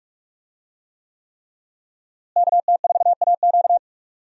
Morse code, heard as a keyed tone, about 46 wpm, 700 Hz. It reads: KT4AQ
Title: USA Call Signs - 5 Characters - Repeated 3x - 45wpm